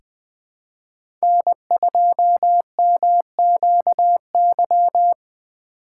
Morse code, heard as a keyed tone, about 20 wpm, 700 Hz. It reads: N2MQY
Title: USA Call Signs - 5 Characters - 2x Character Spacing - 20wpm